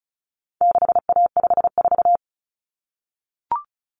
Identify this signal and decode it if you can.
Morse code, heard as a keyed tone, about 34 wpm, 700 Hz. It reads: 6A54
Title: Common Cut Numbers - 4-Digit 35wpm